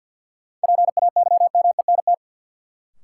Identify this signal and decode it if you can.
Morse code, heard as a keyed tone, about 53 wpm, 700 Hz. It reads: PAYMENT
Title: Sentences from Top 1200 Words - New Word Drill Alt - 50wpm